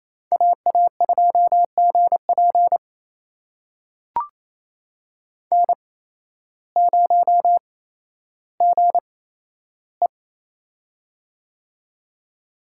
Morse code, heard as a keyed tone, about 28 wpm, spaced as for 13 wpm, 700 Hz. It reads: AA2GP N0GE
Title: USA Call Signs - 8x Character Spacing - 28wpm